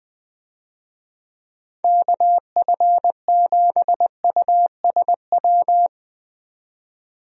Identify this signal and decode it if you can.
Morse code, heard as a keyed tone, about 20 wpm, 700 Hz. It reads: KF7USW
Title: USA Call Signs - 6 Characters - 20wpm